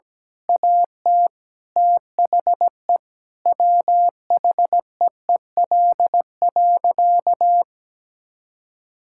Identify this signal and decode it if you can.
Morse code, heard as a keyed tone, about 17 wpm, 700 Hz. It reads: AT THE WHEEL.
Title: Sentences from Top 1000 Words - Review Part 5 - 17wpm